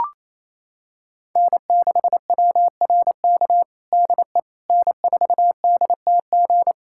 Morse code, heard as a keyed tone, about 28 wpm, 700 Hz. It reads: N6WRK DE N4DTG
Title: USA Callsign DE Callsign 28wpm